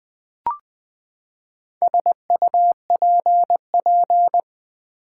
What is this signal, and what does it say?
Morse code, heard as a keyed tone, about 20 wpm, 700 Hz. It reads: SUPP
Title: Build A Word - 4 Letters At A Time - 10-Letter Words - 20wpm